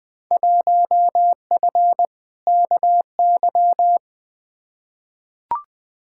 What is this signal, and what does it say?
Morse code, heard as a keyed tone, about 20 wpm, 700 Hz. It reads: 1F KY
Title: Field Day Exchanges Encore - 20wpm